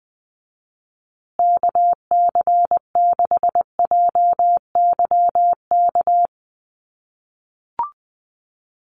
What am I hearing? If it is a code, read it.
Morse code, heard as a keyed tone, about 20 wpm, 700 Hz. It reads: KC6JYK
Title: USA Call Signs Repeated 3x 20wpm